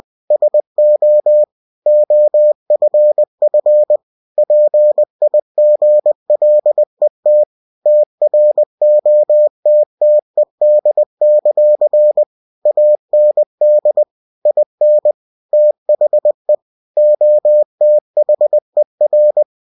Morse code, heard as a keyed tone, about 20 wpm, 600 Hz. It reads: SO OFF PIGLET TROTTED; AND IN THE OTHER
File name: cw-books / winnie-the-pooh / winnie-20wpm-page-0049